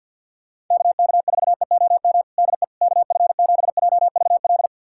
Morse code, heard as a keyed tone, about 51 wpm, 700 Hz. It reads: KK4EOM DE KF7JVL
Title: USA Callsign DE Callsign - 6 Characters - 50wpm